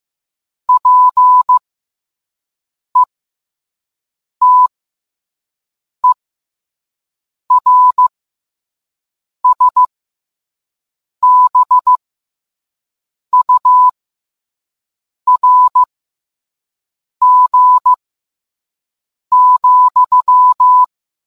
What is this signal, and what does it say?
Morse code, heard as a keyed tone, about 15 wpm, spaced as for 5 wpm, 1000 Hz. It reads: PETERSBURG,